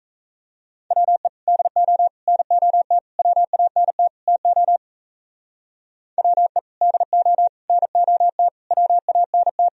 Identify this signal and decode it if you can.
Morse code, heard as a keyed tone, about 34 wpm, 700 Hz. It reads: WE DO NOT WANT TO WE DO NOT WANT
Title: Speed-Racing - Sets of 5 Words 28wpm